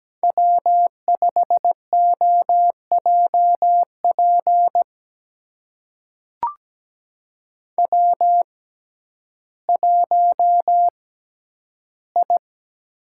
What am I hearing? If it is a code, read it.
Morse code, heard as a keyed tone, about 17 wpm, spaced as for 10 wpm, 700 Hz. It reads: W5OJP W1I
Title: USA Call Signs - 6x Character Spacing - 17wpm